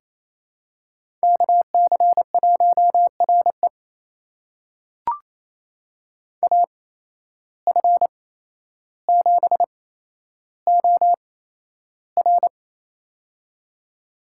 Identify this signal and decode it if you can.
Morse code, heard as a keyed tone, about 28 wpm, spaced as for 12 wpm, 700 Hz. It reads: KC1RE AF7OR
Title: USA Call Signs - 5 Characters - 8x Character Spacing - 28wpm